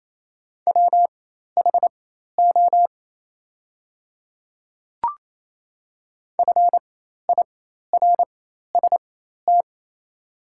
Morse code, heard as a keyed tone, about 28 wpm, spaced as for 13 wpm, 700 Hz. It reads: WHO FIRST
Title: 4x Character Spacing - Top 100 Words - 28wpm